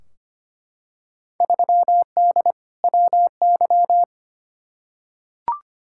Morse code, heard as a keyed tone, about 25 wpm, 700 Hz. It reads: 3D WY
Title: Field Day Exchanges Encore - 25wpm